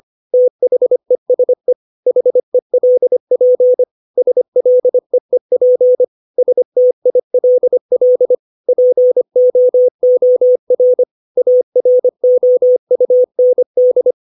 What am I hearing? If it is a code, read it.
Morse code, heard as a keyed tone, about 25 wpm, 500 Hz. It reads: THESE HELP SLEEP STILL POOR AROUND
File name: 632 words / 632 Words 25WPM